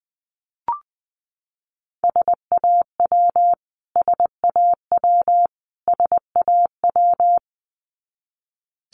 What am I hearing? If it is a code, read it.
Morse code, heard as a keyed tone, about 20 wpm, 700 Hz. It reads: SAW SAW SAW
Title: Top 500 Words Repeated 20wpm